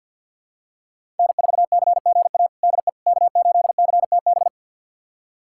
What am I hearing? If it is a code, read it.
Morse code, heard as a keyed tone, about 50 wpm, 700 Hz. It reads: N4CGA DE K8CTB
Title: USA Callsign DE Callsign - 5 Characters - 50wpm